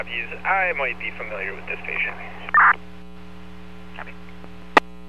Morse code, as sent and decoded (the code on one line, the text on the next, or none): none